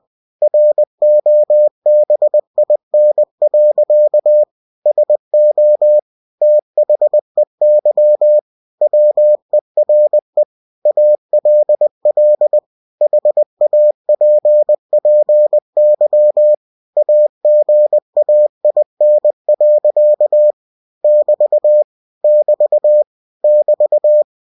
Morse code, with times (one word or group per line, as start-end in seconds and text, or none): 0.4-4.4 ROBIN.
4.9-6.0 SO
6.4-8.4 THEY
8.8-10.4 WERE
10.9-12.6 ALL
13.0-16.5 HAPPY
17.0-20.5 AGAIN.
21.0-21.8 <BT>
22.2-23.0 <BT>
23.4-24.2 <BT>